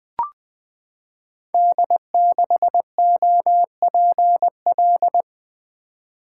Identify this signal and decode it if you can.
Morse code, heard as a keyed tone, about 20 wpm, 700 Hz. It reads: D6OPL
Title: Code Groups - 5 Letter:Numbers - 20wpm